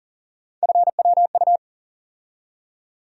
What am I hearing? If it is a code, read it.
Morse code, heard as a keyed tone, about 40 wpm, 700 Hz. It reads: FWU
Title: USA Call Sign Postfixes - 3 Letters 40wpm